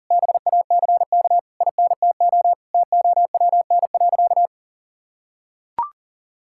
Morse code, heard as a keyed tone, about 40 wpm, 700 Hz. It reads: BACK INTO TOWN.
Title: Sentences from Top 1000 Words - Review Part 6 - 40wpm